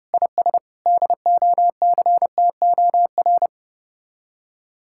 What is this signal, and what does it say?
Morse code, heard as a keyed tone, about 31 wpm, 700 Hz. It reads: IS DOCTOR
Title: Speed-Racing Common QSO Phrases - 25wpm